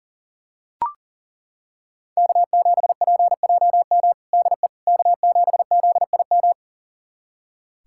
Morse code, heard as a keyed tone, about 40 wpm, 700 Hz. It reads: K7PJM DE K7ZIM
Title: USA Callsign DE Callsign - 5 Characters - 40wpm